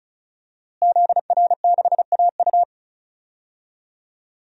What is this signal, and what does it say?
Morse code, heard as a keyed tone, about 36 wpm, 700 Hz. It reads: ZR6AU
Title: DXCC-100 Call Signs - 5 Characters - Repeated 3x - 35wpm